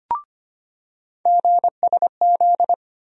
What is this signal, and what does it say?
Morse code, heard as a keyed tone, about 25 wpm, 700 Hz. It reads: GSZ
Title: USA Call Sign Postfixes - 3 Letters 25wpm